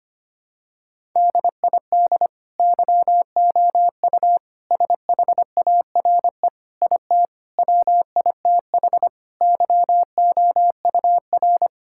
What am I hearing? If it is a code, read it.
Morse code, heard as a keyed tone, about 25 wpm, 700 Hz. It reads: DID YOU SHARE IT WITH YOUR
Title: Sentences from Top 1000 Words - Review Part 6 - 25wpm